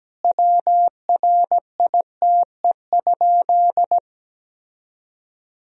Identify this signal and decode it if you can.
Morse code, heard as a keyed tone, about 17 wpm, 700 Hz. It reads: WRITE?
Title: Sentences from Top 1000 Words - Review Part 1 - 17wpm